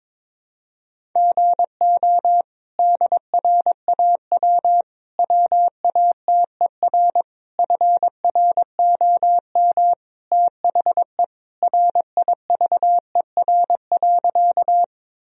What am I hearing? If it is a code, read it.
Morse code, heard as a keyed tone, about 22 wpm, 700 Hz. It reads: GO DRAW WATER FROM THE RIVER.
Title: Sentences from Top 1000 Words - Review Part 1 - 22wpm